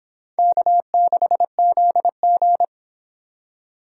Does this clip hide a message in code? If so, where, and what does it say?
Yes: Morse code from 0.4 to 2.6 s, heard as a keyed tone, about 26 wpm, 700 Hz. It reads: K6ZG